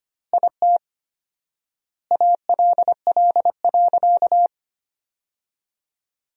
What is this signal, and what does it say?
Morse code, heard as a keyed tone, about 25 wpm, 700 Hz. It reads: IT ALL.
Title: 4x Word-Spacing - Sentences from Top 300 Words - 25wpm